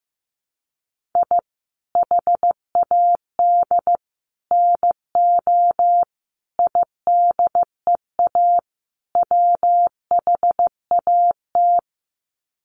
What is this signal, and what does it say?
Morse code, heard as a keyed tone, about 15 wpm, 700 Hz. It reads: I HAD NO IDEA WHAT
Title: Sets of 5 Words Encore 15wpm